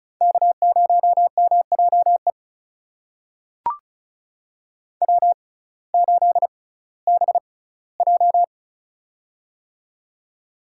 Morse code, heard as a keyed tone, about 34 wpm, spaced as for 19 wpm, 700 Hz. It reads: K0MJE W8BJ
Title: USA Call Signs - 6x Character Spacing - 35wpm